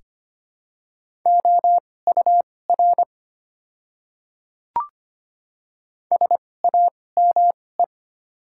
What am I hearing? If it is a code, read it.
Morse code, heard as a keyed tone, about 25 wpm, spaced as for 18 wpm, 700 Hz. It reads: OUR SAME
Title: 2x Character Spacing - Top 200 Words - 25wpm